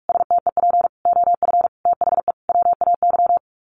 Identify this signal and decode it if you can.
Morse code, heard as a keyed tone, about 44 wpm, 700 Hz. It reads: STEP OF THE WAY
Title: Sets of 5 Words Encore2 45wpm